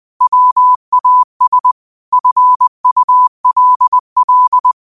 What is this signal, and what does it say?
Morse code, heard as a keyed tone, about 20 wpm, 1000 Hz. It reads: WAS FULL